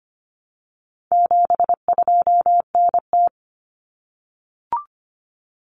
Morse code, heard as a keyed tone, about 25 wpm, 700 Hz. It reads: 72NT
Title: Common Cut Numbers - 4-Digit 25wpm